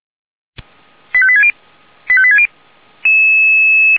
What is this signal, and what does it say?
Five-tone selective call (ZVEI), heard as quick runs of five tones, tones of about 70 ms. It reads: 85860 85860